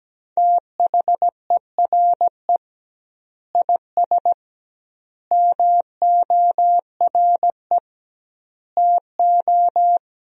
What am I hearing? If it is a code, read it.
Morse code, heard as a keyed tone, about 17 wpm, 700 Hz. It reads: THERE IS MORE TO
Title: 2x Word-Spacing - Sentences from Top 200 Words - 17wpm